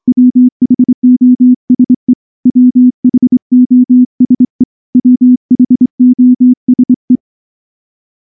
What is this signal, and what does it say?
Morse code, heard as a keyed tone, about 23 wpm, 250 Hz. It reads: WHOSE WHOSE WHOSE
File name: morse / morse-training-25WPM